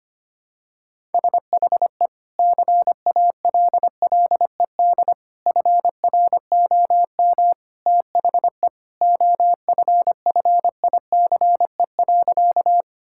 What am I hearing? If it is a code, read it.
Morse code, heard as a keyed tone, about 25 wpm, 700 Hz. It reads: SHE CALLED FROM THE OFFICE.
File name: practice / 25wpm / Sentences with 2-Word Phrases - 25wpm